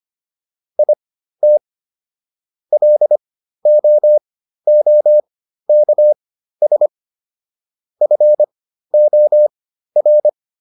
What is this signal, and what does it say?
Morse code, heard as a keyed tone, about 25 wpm, spaced as for 13 wpm, 600 Hz. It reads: IT LOOKS FOR